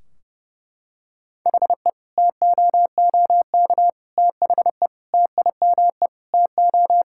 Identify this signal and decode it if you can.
Morse code, heard as a keyed tone, about 32 wpm, 700 Hz. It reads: HE TOOK THE TIME TO
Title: Sentences from Top 1600 Words 30wpm